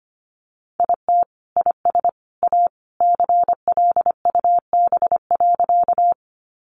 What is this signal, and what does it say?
Morse code, heard as a keyed tone, about 25 wpm, 700 Hz. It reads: IT IS A CLUB.
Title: Sentences from Top 1200 Words 25wpm